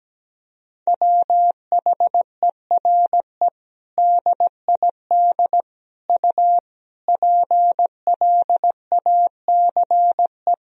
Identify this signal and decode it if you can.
Morse code, heard as a keyed tone, about 17 wpm, 700 Hz. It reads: WHERE DID U PLACE
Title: Sentences from Top 200 Words and Abbreviations 17wpm